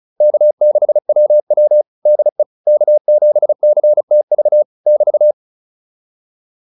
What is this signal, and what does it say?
Morse code, heard as a keyed tone, about 34 wpm, 600 Hz. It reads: K6WW DE K7CTV <BT>